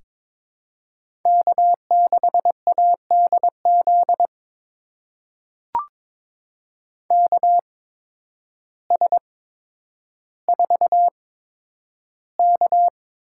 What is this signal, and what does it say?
Morse code, heard as a keyed tone, about 22 wpm, spaced as for 10 wpm, 700 Hz. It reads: K6ADZ KS4K
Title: USA Call Signs - 8x Character Spacing - 22wpm